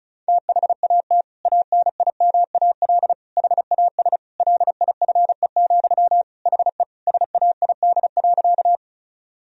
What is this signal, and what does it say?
Morse code, heard as a keyed tone, about 36 wpm, 700 Hz. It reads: THAT ANIMAL HAS LIFE, HE SAID.
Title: Sentences from Top 1000 Words - Review Part 1 - 35wpm